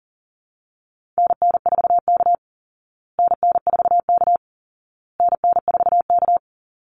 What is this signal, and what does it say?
Morse code, heard as a keyed tone, about 40 wpm, 700 Hz. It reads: NN4K NN4K NN4K
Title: Call Signs Repeated 2x - 40wpm